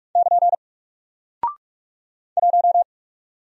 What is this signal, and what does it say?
Morse code, heard as a keyed tone, about 44 wpm, 700 Hz. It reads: <KN> 1
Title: Single Character (Letter:Number:Punctuation:Prosign) Alt 45wpm